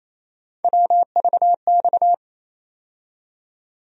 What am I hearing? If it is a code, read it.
Morse code, heard as a keyed tone, about 28 wpm, 700 Hz. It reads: WVX
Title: USA Call Sign Postfixes - 3 Letters 28wpm